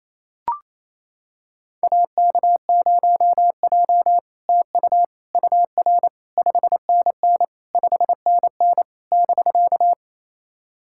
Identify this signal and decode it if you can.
Morse code, heard as a keyed tone, about 28 wpm, 700 Hz. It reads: AK0J TU UR 5NN 5NN <BK>
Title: SOTA QSO Practice 28wpm